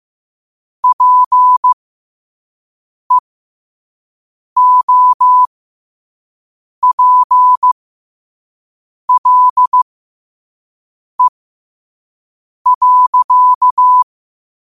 Morse code, heard as a keyed tone, about 15 wpm, spaced as for 5 wpm, 1000 Hz. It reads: PEOPLE.
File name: qotdfiles / 5wpm